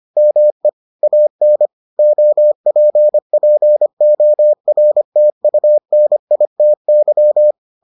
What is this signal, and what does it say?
Morse code, heard as a keyed tone, about 25 wpm, 600 Hz. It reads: ME AN OPPORTUNITY